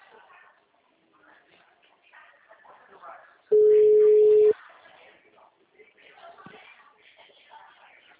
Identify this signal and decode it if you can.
Ringback tone (Europe, 425 Hz), with speech.
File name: out- -2006- - - .27